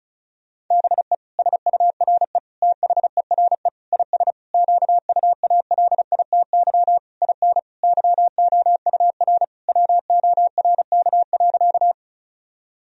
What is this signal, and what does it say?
Morse code, heard as a keyed tone, about 36 wpm, 700 Hz. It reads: BE SURE THERE IS QUALITY IN YOUR WORK.
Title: Sentences from Top 1000 Words - Review Part 4 - 35wpm